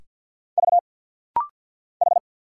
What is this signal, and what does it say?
Morse code, heard as a keyed tone, about 50 wpm, 700 Hz. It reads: V H
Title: Single Letters - Rapid-Fire 50wpm